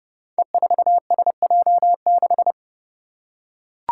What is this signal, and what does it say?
Morse code, heard as a keyed tone, about 31 wpm, 700 Hz. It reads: E4SJ6
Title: Five Challenging Letter-Number Combinations 30wpm